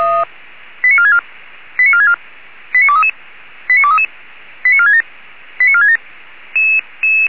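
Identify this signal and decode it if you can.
Five-tone selective call (ZVEI), heard as quick runs of five tones, tones of about 70 ms. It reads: 89464 89464 89233 89233 89467 89467
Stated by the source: ZVEI five-tone selective call received by SDR.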